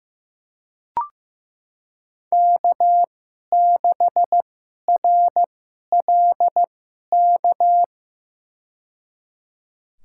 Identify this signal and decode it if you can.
Morse code, heard as a keyed tone, about 16 wpm, spaced as for 11 wpm, 700 Hz. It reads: K6RLK
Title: USA Call Signs - 2x Character Spacing - 15wpm